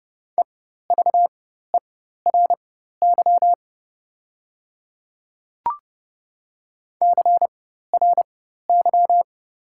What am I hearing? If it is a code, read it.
Morse code, heard as a keyed tone, about 30 wpm, spaced as for 14 wpm, 690 Hz. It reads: EVERY CRY
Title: 4x Character Spacing - Top 500 Words - 30wpm